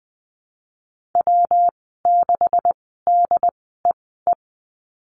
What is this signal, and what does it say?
Morse code, heard as a keyed tone, about 20 wpm, spaced as for 15 wpm, 700 Hz. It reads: W6DEE
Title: 147 Lesson 24g - Letter y - Call Signs 20wpm